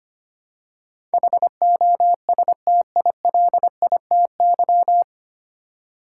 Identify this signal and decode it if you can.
Morse code, heard as a keyed tone, about 25 wpm, 700 Hz. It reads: HOSTILITY
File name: Nine Letter Words - 25wpm